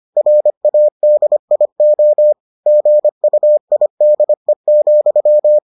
Morse code, heard as a keyed tone, about 25 wpm, 600 Hz. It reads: RADIO GUIDE,